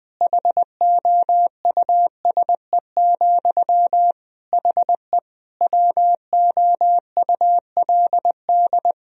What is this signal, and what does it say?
Morse code, heard as a keyed tone, about 20 wpm, 700 Hz. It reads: HOUSE, HE WOULD
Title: Sentences from Top 1000 Words - Review Part 2 - 20wpm